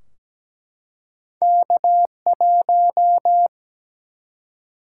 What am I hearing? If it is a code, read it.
Morse code, heard as a keyed tone, about 17 wpm, 700 Hz. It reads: K1
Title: Single-Letter Single-Number 17wpm